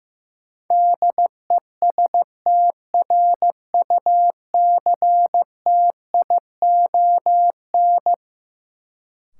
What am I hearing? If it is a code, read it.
Morse code, heard as a keyed tone, about 15 wpm, 700 Hz. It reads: DESTRUCTION
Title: Sets of 5 Words Encore - New Word Drill - 15wpm